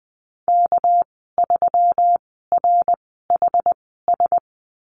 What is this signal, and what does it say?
Morse code, heard as a keyed tone, about 20 wpm, spaced as for 14 wpm, 700 Hz. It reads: K3RHS
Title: 207 Lesson 33g - Letter j - Call Signs 20wpm